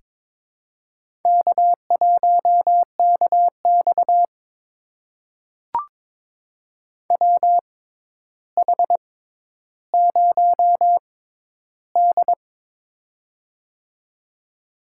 Morse code, heard as a keyed tone, about 22 wpm, spaced as for 11 wpm, 700 Hz. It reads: K1KX WH0D
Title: USA Call Signs - 4 Characters - 6x Character Spacing - 22wpm